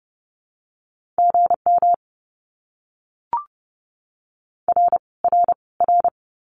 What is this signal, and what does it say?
Morse code, heard as a keyed tone, about 30 wpm, 700 Hz. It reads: GM R R R
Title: Abbreviations Repeated 3x - 30wpm